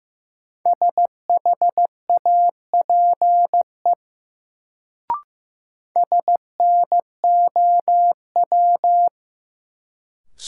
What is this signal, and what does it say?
Morse code, heard as a keyed tone, about 15 wpm, 700 Hz. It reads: SHAPE SNOW
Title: Sentences from Top 500 Words - New Word Drill - 15wpm